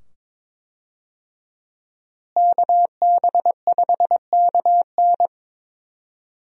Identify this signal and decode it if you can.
Morse code, heard as a keyed tone, about 22 wpm, 700 Hz. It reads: KB5KN
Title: USA Call Signs - 5 Characters - Repeated 2x - 22wpm